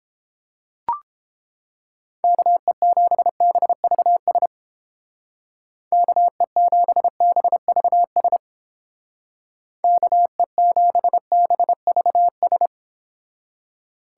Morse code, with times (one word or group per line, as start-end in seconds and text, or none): 2.2-4.5 KE7BVS
5.9-8.4 KE7BVS
9.8-12.7 KE7BVS